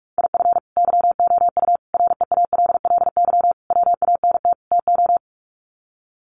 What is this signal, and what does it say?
Morse code, heard as a keyed tone, about 44 wpm, 700 Hz. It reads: IF YOU REALLY WANT TO
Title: Sets of 5 Words Encore2 45wpm